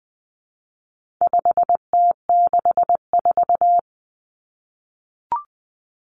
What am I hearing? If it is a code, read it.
Morse code, heard as a keyed tone, about 20 wpm, 700 Hz. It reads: 5T64
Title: Common Cut Numbers - 4-Digit 20wpm